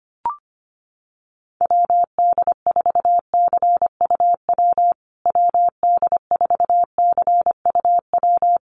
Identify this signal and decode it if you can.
Morse code, heard as a keyed tone, about 25 wpm, 700 Hz. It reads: WD4CUW WD4CUW
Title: USA Call Signs Repeated 3x 25wpm